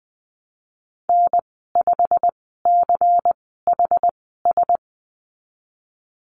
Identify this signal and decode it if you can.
Morse code, heard as a keyed tone, about 20 wpm, spaced as for 15 wpm, 700 Hz. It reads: N5CHS
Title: 147 Lesson 24g - Letter y - Call Signs 20wpm